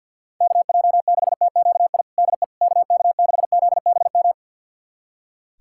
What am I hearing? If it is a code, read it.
Morse code, heard as a keyed tone, about 46 wpm, 700 Hz. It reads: KJ6TQI DE KK6ZBM
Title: USA Callsign DE Callsign 50wpm